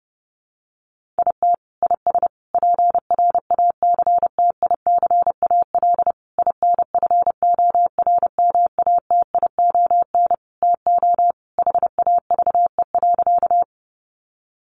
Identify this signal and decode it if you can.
Morse code, heard as a keyed tone, about 30 wpm, 700 Hz. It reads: IT IS PRACTICAL INFORMATION TO HAVE.
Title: Sentences from Top 1400 Words 30wpm